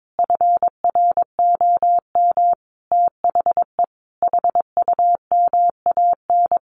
Morse code, heard as a keyed tone, about 22 wpm, 700 Hz. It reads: FROM THE HUMAN
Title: Sentences from Top 1400 Words 22wpm